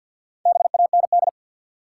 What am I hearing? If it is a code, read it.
Morse code, heard as a keyed tone, about 53 wpm, 700 Hz. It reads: BAND
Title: Four Letter Words - 50wpm